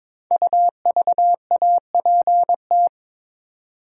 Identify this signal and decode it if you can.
Morse code, heard as a keyed tone, about 22 wpm, 700 Hz. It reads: UVAPT
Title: Code Groups - 5 Letter:Numbers - 22wpm